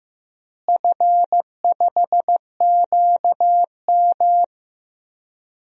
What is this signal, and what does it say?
Morse code, heard as a keyed tone, about 15 wpm, 700 Hz. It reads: F5QM
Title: DXCC-100 Call Signs - 4 Characters - Repeated 3x - 15wpm